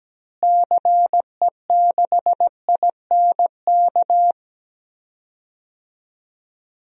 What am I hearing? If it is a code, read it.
Morse code, heard as a keyed tone, about 17 wpm, 700 Hz. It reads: CE6INK
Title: DXCC-100 Call Signs - 6 Characters - Repeated 2x - 17wpm